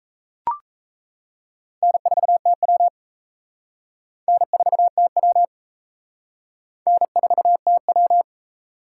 Morse code, heard as a keyed tone, about 37 wpm, 700 Hz. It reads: N4TW N4TW N4TW